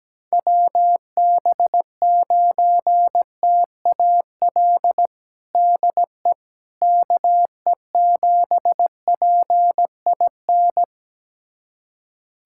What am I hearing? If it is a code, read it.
Morse code, heard as a keyed tone, about 17 wpm, 700 Hz. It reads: WB9TAL DE KE7PIN
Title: USA Callsign DE Callsign - 6 Characters - 17wpm